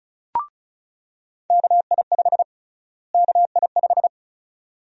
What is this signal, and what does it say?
Morse code, heard as a keyed tone, about 36 wpm, 700 Hz. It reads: KI5 KI5
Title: USA Call Sign Prefixes - Repeated 2x - 35wpm